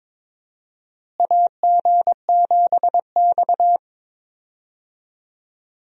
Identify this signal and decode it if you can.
Morse code, heard as a keyed tone, about 22 wpm, 700 Hz. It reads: AG7X
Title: USA Call Signs - 4 Characters - Repeated 3x - 22wpm